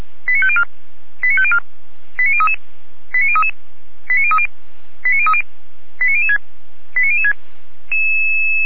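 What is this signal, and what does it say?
Five-tone selective call (ZVEI), heard as quick runs of five tones, tones of about 70 ms. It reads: 89504 89504 89033 89033 89030 89030 89006 89006
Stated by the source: ZVEI five-tone call 89011 received by SDR.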